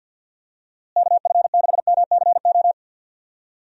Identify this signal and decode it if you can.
Morse code, heard as a keyed tone, about 47 wpm, 700 Hz. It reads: KF6MCO
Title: USA Call Signs - 6 Characters - Repeated 3x - 50wpm